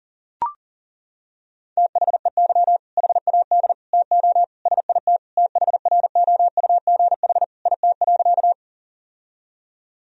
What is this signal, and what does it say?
Morse code, heard as a keyed tone, about 40 wpm, 700 Hz. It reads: THEY HAD TO SIT THROUGH IT.
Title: Sentences from Top 1000 Words - Review Part 1 - 40wpm